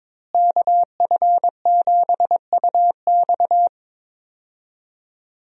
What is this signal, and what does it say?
Morse code, heard as a keyed tone, about 22 wpm, 700 Hz. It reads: KF7UX
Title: USA Call Signs - 5 Characters - Repeated 3x - 22wpm